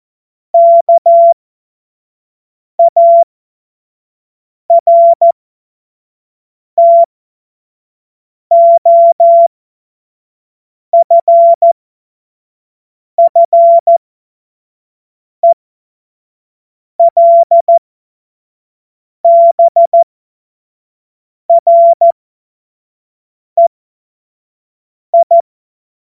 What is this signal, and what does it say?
Morse code, heard as a keyed tone, about 14 wpm, spaced as for 5 wpm, 680 Hz. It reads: KARTOFFELBREI
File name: Morsekurs / Witze / Witz 11. Mai 2021, 25 F70